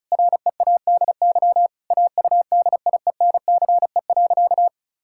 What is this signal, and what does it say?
Morse code, heard as a keyed tone, about 36 wpm, 700 Hz. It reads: READY AUDIENCE.
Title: Sentences from Top 1000 Words - Review Part 5 - 35wpm